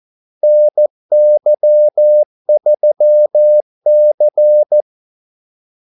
Morse code, heard as a keyed tone, about 14 wpm, 600 Hz. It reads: NY3C
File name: Callsigns-14wpm